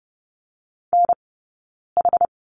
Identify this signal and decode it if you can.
Morse code, heard as a keyed tone, about 30 wpm, spaced as for 9 wpm, 690 Hz. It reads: NH
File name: Lesson 17d - Letter u - Two Characters 30wpm